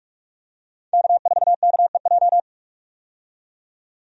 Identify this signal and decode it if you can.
Morse code, heard as a keyed tone, about 46 wpm, 700 Hz. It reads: K4KEJ
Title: USA Call Signs - 5 Characters - 45wpm